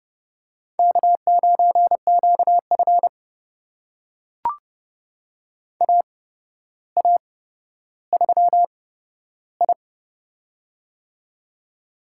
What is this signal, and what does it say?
Morse code, heard as a keyed tone, about 30 wpm, spaced as for 13 wpm, 700 Hz. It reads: K9QF AA3I
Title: USA Call Signs - 4 Characters - 8x Character Spacing - 30wpm